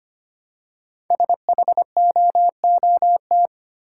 Morse code, heard as a keyed tone, about 25 wpm, 700 Hz. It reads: SHOOT
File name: 5 Letter Words - First 3 Letters Given - 25wpm